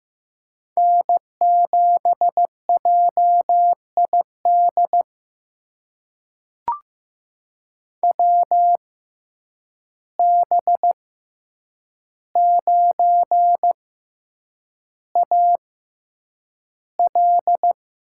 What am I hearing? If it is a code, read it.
Morse code, heard as a keyed tone, about 15 wpm, spaced as for 8 wpm, 700 Hz. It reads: N7JID WB9AL
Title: USA Call Signs - 6x Character Spacing - 15wpm